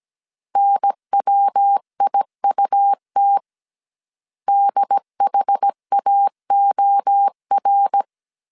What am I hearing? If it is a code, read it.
Morse code, heard as a keyed tone, about 17 wpm, 800 Hz. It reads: NWIUT DHAOR